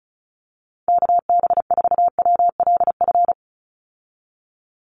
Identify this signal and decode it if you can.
Morse code, heard as a keyed tone, about 34 wpm, 700 Hz. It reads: KB4WLF
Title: Call Signs Repeated 3x - 35wpm